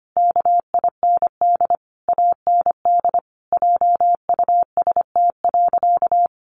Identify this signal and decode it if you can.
Morse code, heard as a keyed tone, about 25 wpm, 700 Hz. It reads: KIND AND JUST.
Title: Sentences from Top 800 Words 25wpm